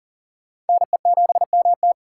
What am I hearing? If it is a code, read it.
Morse code, heard as a keyed tone, about 50 wpm, 700 Hz. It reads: NE7MT
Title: Speed Racing - USA 2x2 Call Signs 40wpm